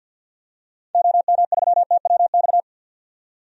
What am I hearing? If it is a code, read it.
Morse code, heard as a keyed tone, about 51 wpm, 700 Hz. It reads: OM3TWX
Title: DXCC-100 Call Signs - 6 Characters - 50wpm